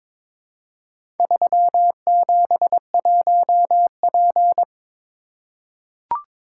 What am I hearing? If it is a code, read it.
Morse code, heard as a keyed tone, about 22 wpm, 700 Hz. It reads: 371P
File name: practice / 22wpm / Four Letter-Number Combinations - 22wpm